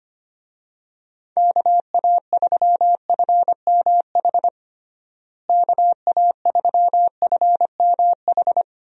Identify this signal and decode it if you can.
Morse code, heard as a keyed tone, about 25 wpm, 700 Hz. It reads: KA3FMH KA3FMH
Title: USA Call Signs - 6 Characters - Repeated 2x - 25wpm